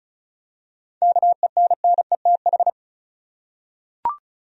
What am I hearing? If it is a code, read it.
Morse code, heard as a keyed tone, about 34 wpm, 700 Hz. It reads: KENNETH
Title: USA Common Names - 35wpm